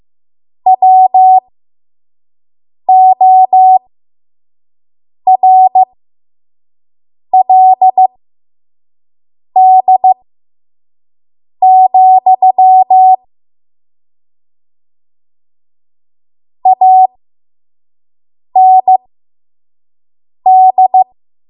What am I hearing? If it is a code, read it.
Morse code, heard as a keyed tone, about 15 wpm, spaced as for 5 wpm, 750 Hz. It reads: WORLD, AND
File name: Morse / 05WPM